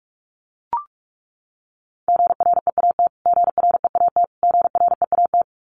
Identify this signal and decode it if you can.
Morse code, heard as a keyed tone, about 44 wpm, 700 Hz. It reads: GREAT GREAT GREAT
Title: Top 500 Words Repeated 45wpm